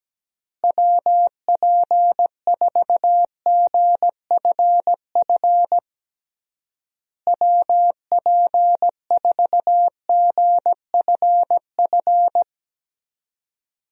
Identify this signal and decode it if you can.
Morse code, heard as a keyed tone, about 17 wpm, 700 Hz. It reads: WP4GFF WP4GFF
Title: USA Call Signs - 6 Characters - Repeated 3x - 17wpm